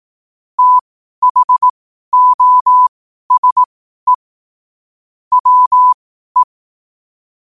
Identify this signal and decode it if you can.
Morse code, heard as a keyed tone, about 18 wpm, spaced as for 12 wpm, 1000 Hz. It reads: THOSE WE
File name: qotdfiles / 12wpm